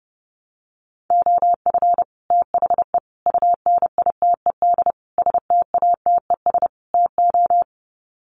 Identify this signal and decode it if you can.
Morse code, heard as a keyed tone, about 30 wpm, 700 Hz. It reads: OF THE UNITED STATES TO
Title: Sets of 5 Words Encore 30wpm